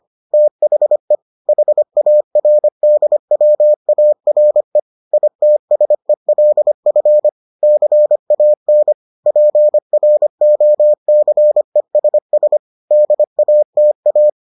Morse code, heard as a keyed tone, about 25 wpm, 600 Hz. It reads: THE HARDWARE ITSELF CAN PROCESS DATA